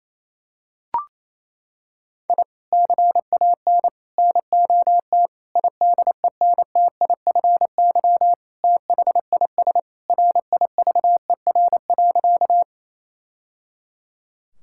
Morse code, heard as a keyed tone, about 27 wpm, 700 Hz. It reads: I CAN NOT IDENTIFY THIS RIVER.
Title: Sentences from Top 1000 Words - Review Part 6 - 28wpm